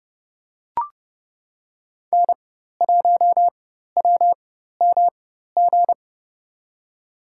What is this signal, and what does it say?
Morse code, heard as a keyed tone, about 32 wpm, spaced as for 14 wpm, 700 Hz. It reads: N1WMG
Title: USA Call Signs - 5 Characters - 4x Character Spacing - 30wpm